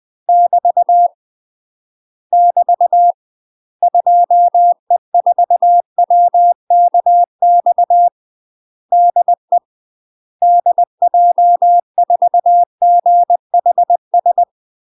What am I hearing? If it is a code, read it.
Morse code, heard as a keyed tone, about 20 wpm, 700 Hz. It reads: <BT> <BT> 2E4WKX DE DJ4GHS
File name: UK MASTER QSOs TRAINING/20WPM